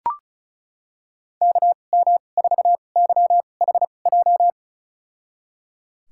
Morse code, heard as a keyed tone, about 36 wpm, spaced as for 25 wpm, 700 Hz. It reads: KM4YHJ
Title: USA Call Signs - 2x Character Spacing - 35wpm